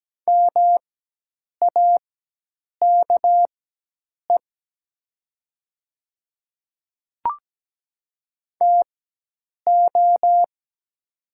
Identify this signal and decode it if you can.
Morse code, heard as a keyed tone, about 17 wpm, spaced as for 8 wpm, 700 Hz. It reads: MAKE TO
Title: 4x Character Spacing - Top 100 Words - 17wpm